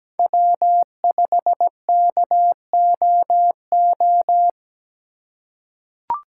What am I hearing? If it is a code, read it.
Morse code, heard as a keyed tone, about 17 wpm, 700 Hz. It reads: W5KOO
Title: USA Call Signs - 5 Characters - 2x Character Spacing - 17wpm